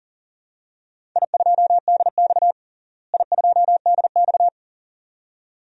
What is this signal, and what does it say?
Morse code, heard as a keyed tone, about 38 wpm, 700 Hz. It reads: I2DX I2DX
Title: DXCC-100 Call Signs - 4 Characters - Repeated 2x - 40wpm